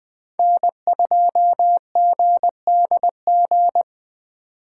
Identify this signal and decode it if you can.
Morse code, heard as a keyed tone, about 20 wpm, 700 Hz. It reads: N2GDG
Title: Speed Racing - USA 1x3 Call Signs 17wpm